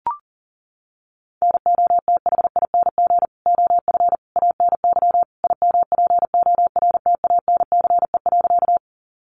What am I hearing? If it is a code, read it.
Morse code, heard as a keyed tone, about 40 wpm, 700 Hz. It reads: NOTHING OF ANY IMPORTANCE.
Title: Sentences from Top 1200 Words 40wpm